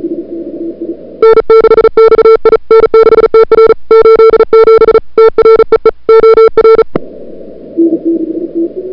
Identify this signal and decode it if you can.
Morse code, heard as a keyed tone, about 37 wpm, 440 Hz. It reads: N6XI N6TR 87 TREE OR
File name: CWSprint / February 2007 CW Sprint / 01-2007 CW Sprint Hour1